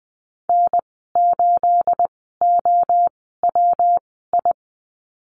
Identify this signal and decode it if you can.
Morse code, heard as a keyed tone, about 20 wpm, spaced as for 15 wpm, 700 Hz. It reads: N8OWI
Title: 238 Lesson 38g - Letter q - Call Signs 20wpm